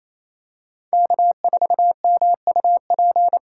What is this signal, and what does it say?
Morse code, heard as a keyed tone, about 28 wpm, 700 Hz. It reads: K4MUP
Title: USA Call Signs - 2x Character Spacing - 28wpm